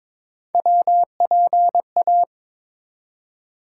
Morse code, heard as a keyed tone, about 21 wpm, 700 Hz. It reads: WPA
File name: practice / 22wpm / ARRL Sections v2 - 22wpm